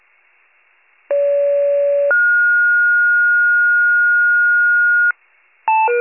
Two-tone page, heard as a one-second tone, then a three-second tone, about 570 then 1430 Hz. It is heard from 1.1 to 5.1 s.